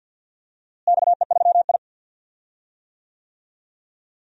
Morse code, heard as a keyed tone, about 53 wpm, 700 Hz. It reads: XE3I
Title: DXCC-100 Call Signs - 4 Characters - Repeated 2x - 50wpm